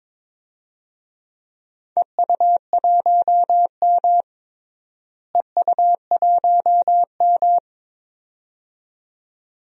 Morse code, heard as a keyed tone, about 21 wpm, 700 Hz. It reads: EU1M EU1M
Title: DXCC-100 Call Signs - 4 Characters - Repeated 2x - 22wpm